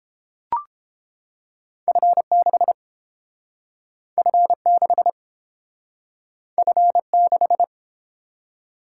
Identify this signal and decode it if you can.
Morse code, heard as a keyed tone, about 32 wpm, 700 Hz. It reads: F6 F6 F6